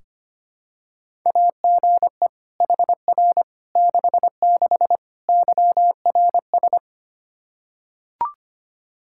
Morse code, heard as a keyed tone, about 25 wpm, 700 Hz. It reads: AGE HR 66 YRS
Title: Practice QSO - Exchange 3 - More Info - 25wpm